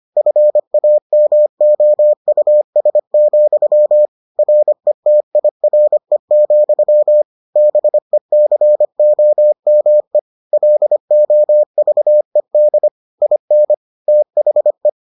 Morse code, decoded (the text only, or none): FAMOUS, RETIRE, BECOME LOVED IN THE